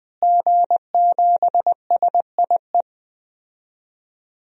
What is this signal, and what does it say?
Morse code, heard as a keyed tone, about 20 wpm, 700 Hz. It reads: G7SIE
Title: DXCC-100 Call Signs - 5 Characters - Repeated 2x - 20wpm